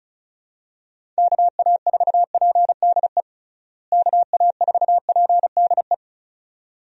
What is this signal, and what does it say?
Morse code, heard as a keyed tone, about 36 wpm, 700 Hz. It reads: KA4PDE KA4PDE
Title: USA Call Signs - 6 Characters - Repeated 2x - 35wpm